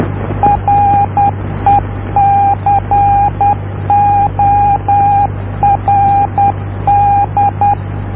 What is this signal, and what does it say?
Morse code, heard as a keyed tone, about 10 wpm, 810 Hz, with radio noise. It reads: RECORD